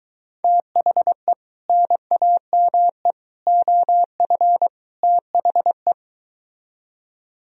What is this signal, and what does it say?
Morse code, heard as a keyed tone, about 23 wpm, 700 Hz. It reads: THE NAME OF THE